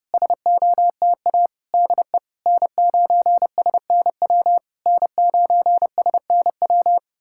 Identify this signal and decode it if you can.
Morse code, heard as a keyed tone, about 32 wpm, 700 Hz. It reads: SOTA DE N9SNW N9SNW
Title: SOTA QSO Practice 30wpm